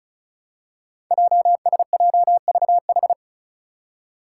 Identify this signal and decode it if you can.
Morse code, heard as a keyed tone, about 36 wpm, 700 Hz. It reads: JSJVH
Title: Five Challenging Letter-Number Combinations 35wpm